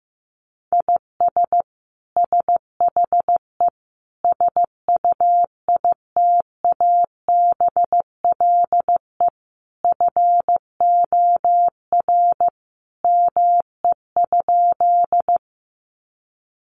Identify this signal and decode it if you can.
Morse code, heard as a keyed tone, about 15 wpm, 700 Hz. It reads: IS SHE SUITABLE FOR ME?
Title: Sentences from Top 1400 Words 15wpm